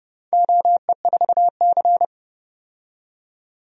Morse code, heard as a keyed tone, about 31 wpm, 700 Hz. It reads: OE4C
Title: DXCC-100 Call Signs - 4 Characters - 30wpm